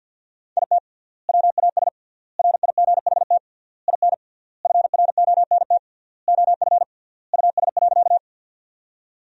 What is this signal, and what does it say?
Morse code, heard as a keyed tone, about 52 wpm, 700 Hz. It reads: IT WAS RIGHT IN FRONT OF US.
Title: 3x Word-Spacing - Sentences from Top 500 Words - 50wpm